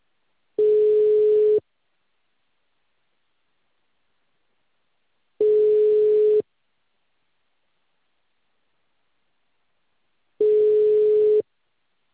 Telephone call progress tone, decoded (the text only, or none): ringback tone